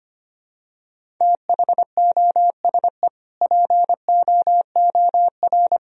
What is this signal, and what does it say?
Morse code, heard as a keyed tone, about 25 wpm, 700 Hz. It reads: THOSE POOR
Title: Sentences from Top 1000 Words - Review Part 1 - 25wpm